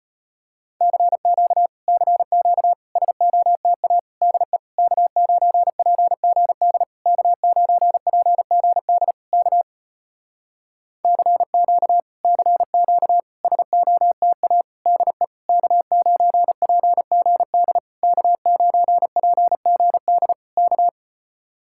Morse code, heard as a keyed tone, about 34 wpm, 700 Hz. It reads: CQ CQ SOTA DE K9PGD K9PGD K CQ CQ SOTA DE K9PGD K9PGD K